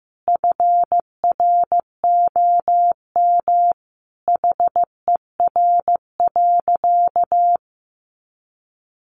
Morse code, heard as a keyed tone, about 15 wpm, 700 Hz. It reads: FROM HER.
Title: Sentences from Top 900 Words 15wpm